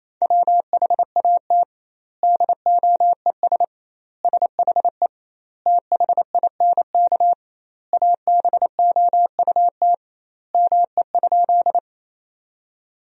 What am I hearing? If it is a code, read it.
Morse code, heard as a keyed tone, about 28 wpm, 700 Hz. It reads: WHAT DOES SHE THINK ABOUT ME?
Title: 2x Word-Spacing - Sentences from Top 200 Words - 28wpm